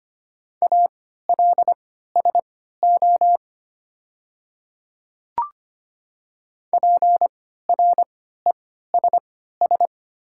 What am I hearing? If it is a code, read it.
Morse code, heard as a keyed tone, about 25 wpm, spaced as for 14 wpm, 690 Hz. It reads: ALSO PRESS
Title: 3x Character Spacing - Top 500 Words - 25wpm